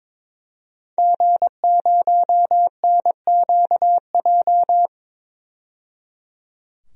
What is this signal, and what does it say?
Morse code, heard as a keyed tone, about 21 wpm, 700 Hz. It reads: G0NQJ
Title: DXCC-100 Call Signs - 5 Characters - 22wpm